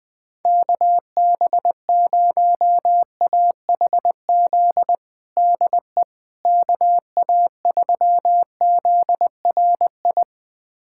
Morse code, heard as a keyed tone, about 20 wpm, 700 Hz. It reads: KB0AHZ DE KA3ZRI
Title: USA Callsign DE Callsign - 6 Characters - 20wpm